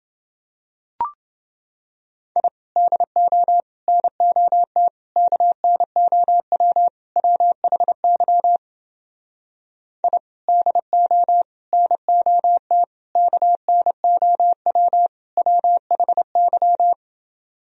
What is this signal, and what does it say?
Morse code, heard as a keyed tone, about 29 wpm, 700 Hz. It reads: I DO NOT KNOW WHY I DO NOT KNOW WHY